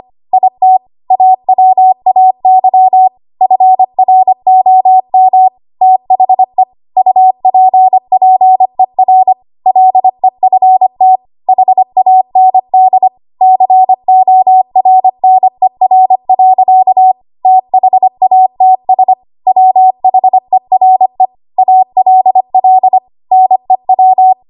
Morse code, heard as a keyed tone, about 25 wpm, 750 Hz. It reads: IT AWAY FROM THE UPPER LEFT HAND CORNER. THATS WHERE ALL NEW
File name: Archive / 25 WPM / 25WPM